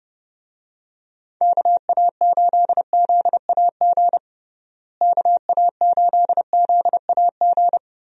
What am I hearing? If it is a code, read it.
Morse code, heard as a keyed tone, about 31 wpm, 700 Hz. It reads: KA8ZAG KA8ZAG
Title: USA Call Signs - 6 Characters - Repeated 2x - 30wpm